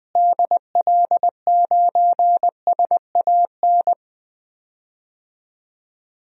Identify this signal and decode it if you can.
Morse code, heard as a keyed tone, about 21 wpm, 700 Hz. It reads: DL9SAN
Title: DXCC-100 Call Signs - 6 Characters - Repeated 2x - 20wpm